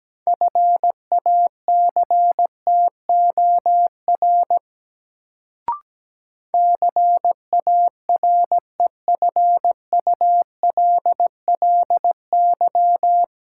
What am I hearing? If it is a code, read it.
Morse code, heard as a keyed tone, about 17 wpm, 700 Hz. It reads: FACTOR CAREFULLY
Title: Sentences from Top 1300 Words - New Word Drill - 17wpm